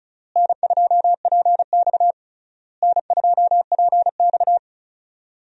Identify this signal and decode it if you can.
Morse code, heard as a keyed tone, about 36 wpm, 700 Hz. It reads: N2PX N2PX
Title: USA Call Signs - 4 Characters - Repeated 3x - 35wpm